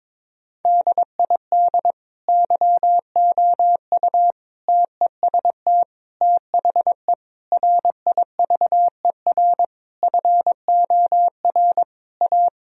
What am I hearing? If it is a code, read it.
Morse code, heard as a keyed tone, about 21 wpm, 700 Hz. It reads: DID YOU TEST THE RIVER FOR A
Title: Sentences from Top 1000 Words - Review Part 4 - 22wpm